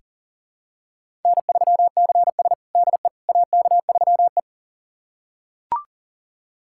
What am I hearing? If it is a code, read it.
Morse code, heard as a keyed tone, about 40 wpm, 700 Hz. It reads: N3CS DE AK3E
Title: USA Callsign DE Callsign - 4 Characters - 40wpm